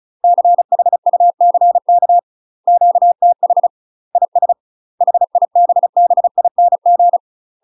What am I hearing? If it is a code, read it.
Morse code, heard as a keyed tone, about 35 wpm, 700 Hz. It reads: CHUCK QTH IS HIBBING
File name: UK MASTER QSOs TRAINING / 35WPM / UK MASTER QSOs 0011